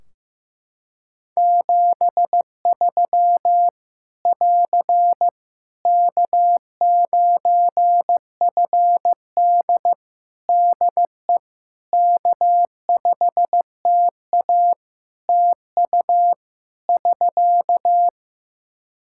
Morse code, heard as a keyed tone, about 16 wpm, 700 Hz. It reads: 73 <AR> K9FD DE K5TA TU <SK>
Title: K1USN SST QSO Practice 15wpm